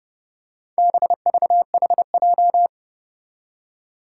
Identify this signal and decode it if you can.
Morse code, heard as a keyed tone, about 30 wpm, 700 Hz. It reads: BVHJ
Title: Four Challenging Letter-Number Combinations 30wpm